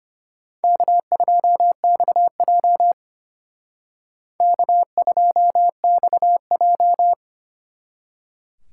Morse code, heard as a keyed tone, about 33 wpm, 700 Hz. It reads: K2XJ K2XJ